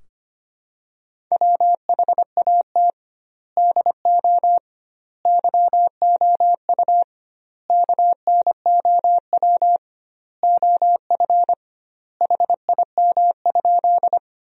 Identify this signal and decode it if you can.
Morse code, heard as a keyed tone, about 25 wpm, 700 Hz. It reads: WHAT DO YOU KNOW OF HIM?
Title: 2x Word-Spacing - Sentences from Top 100 Words - 25wpm